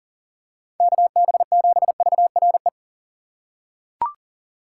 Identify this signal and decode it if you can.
Morse code, heard as a keyed tone, about 38 wpm, 700 Hz. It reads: KB7VRE
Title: USA Call Signs - 6 Characters - 2x Character Spacing - 40wpm